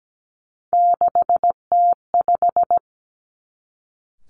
Morse code, heard as a keyed tone, about 17 wpm, 700 Hz. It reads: 6T5
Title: Three Challenging Letter-Number Combinations 17wpm